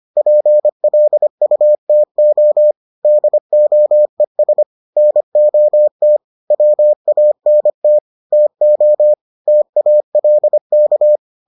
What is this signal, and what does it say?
Morse code, heard as a keyed tone, about 25 wpm, 600 Hz. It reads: PLUTO DOES NOT WANT TO TALK